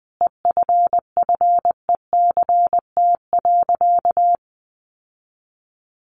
Morse code, heard as a keyed tone, about 20 wpm, 700 Hz. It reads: EFFECT.
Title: Sentences from Top 1400 Words 20wpm